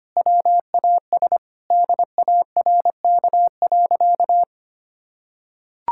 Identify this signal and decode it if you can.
Morse code, heard as a keyed tone, about 25 wpm, 700 Hz. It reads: WAS DARK.
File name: practice / 25wpm / Sentences with 4-Word Phrases - 25wpm